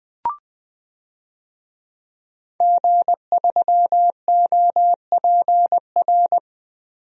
Morse code, heard as a keyed tone, about 21 wpm, 700 Hz. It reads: G3OPR
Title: DXCC-100 Call Signs - 5 Characters - 20wpm